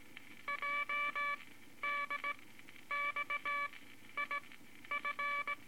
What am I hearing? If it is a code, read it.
Morse code, heard as a keyed tone, about 17 wpm, spaced as for 11 wpm, 2300 Hz, with noise. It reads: JDXIF
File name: Morse Code Records / 18 B LCM 165 Ten Words Per Minute